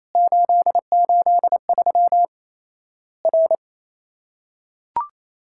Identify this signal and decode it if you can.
Morse code, heard as a keyed tone, about 28 wpm, 700 Hz. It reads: 883 R
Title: Worked All Europe DX Contest - QTC Exchange - 28wpm